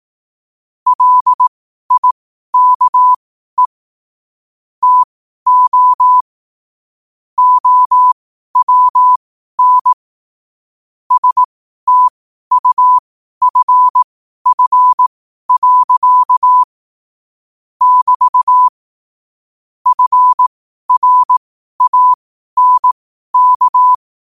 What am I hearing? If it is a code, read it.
Morse code, heard as a keyed tone, about 18 wpm, spaced as for 12 wpm, 1000 Hz. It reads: LIKE TO OWN STUFF. <BT> FRANK
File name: qotdfiles / 12wpm